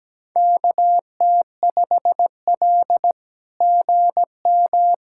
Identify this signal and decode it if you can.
Morse code, heard as a keyed tone, about 17 wpm, 700 Hz. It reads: KT5L GM
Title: SOTA QSO Practice 17wpm